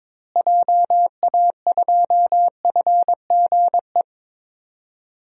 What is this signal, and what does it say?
Morse code, heard as a keyed tone, about 22 wpm, 700 Hz. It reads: JA2FGE
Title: DXCC-100 Call Signs - 6 Characters - Repeated 2x - 22wpm